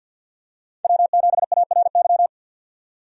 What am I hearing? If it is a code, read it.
Morse code, heard as a keyed tone, about 50 wpm, 700 Hz. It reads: W7ARY
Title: USA Call Signs - 5 Characters - 50wpm